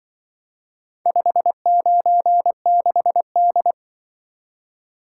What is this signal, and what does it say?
Morse code, heard as a keyed tone, about 25 wpm, 700 Hz. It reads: 596D
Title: Speed-Racing - Four Letter-Number Combinations - 20wpm